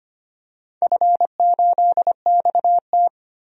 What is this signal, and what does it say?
Morse code, heard as a keyed tone, about 25 wpm, 700 Hz. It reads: F8XT
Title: DXCC-100 Call Signs - 4 Characters - 25wpm